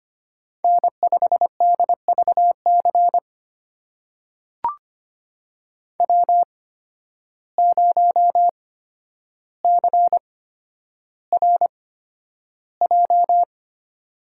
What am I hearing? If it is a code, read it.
Morse code, heard as a keyed tone, about 25 wpm, spaced as for 11 wpm, 700 Hz. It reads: N5DVC W0CRJ
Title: USA Call Signs - 5 Characters - 8x Character Spacing - 25wpm